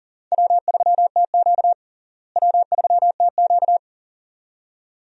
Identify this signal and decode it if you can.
Morse code, heard as a keyed tone, about 37 wpm, 700 Hz. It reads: W3TQ W3TQ
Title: USA Call Signs - 4 Characters - Repeated 2x - 40wpm